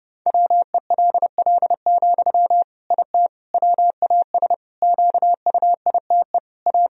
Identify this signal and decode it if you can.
Morse code, heard as a keyed tone, about 32 wpm, 700 Hz. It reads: WELL, IT WAS QUITE A
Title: Sentences from Top 1000 Words - Review Part 3 - 30wpm